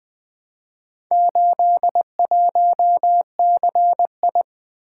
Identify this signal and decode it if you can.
Morse code, heard as a keyed tone, about 20 wpm, 700 Hz. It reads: 81CI
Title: Four Letter-Number Combinations - 20wpm